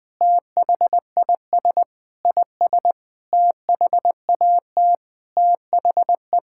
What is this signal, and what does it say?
Morse code, heard as a keyed tone, about 20 wpm, 700 Hz. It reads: THIS IS THAT THE
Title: Sentences with 7-Word Phrases - 20wpm